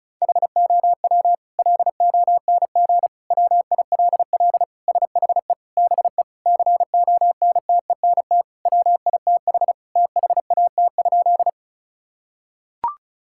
Morse code, heard as a keyed tone, about 34 wpm, 700 Hz. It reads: HOW LONG WILL SHE BE CONTENT WITH THAT?
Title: Sentences from Top 1000 Words - Review Part 6 - 35wpm